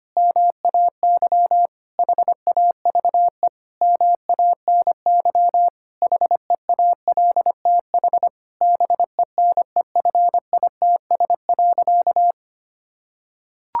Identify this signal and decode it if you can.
Morse code, heard as a keyed tone, about 25 wpm, 700 Hz. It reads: MAY HAVE MANY HEALTH BENEFITS.
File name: practice / 25wpm / Difficult Sentences with Hints - 25wpm